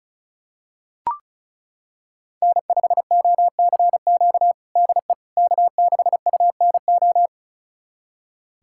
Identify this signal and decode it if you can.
Morse code, heard as a keyed tone, about 34 wpm, 700 Hz. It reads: N5OCQ DE K6UNO
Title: USA Callsign DE Callsign - 5 Characters - 35wpm